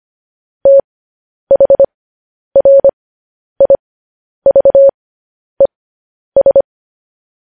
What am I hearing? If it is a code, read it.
Morse code, heard as a keyed tone, about 25 wpm, spaced as for 10 wpm, 550 Hz. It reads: THRIVES
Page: News Headlines in Morse Code at 10 WPM